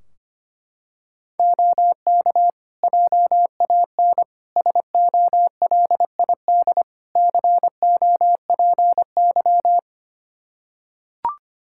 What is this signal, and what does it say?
Morse code, heard as a keyed tone, about 25 wpm, 700 Hz. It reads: OK JAN SOLID COPY
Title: K1USN SST QSO Practice 25wpm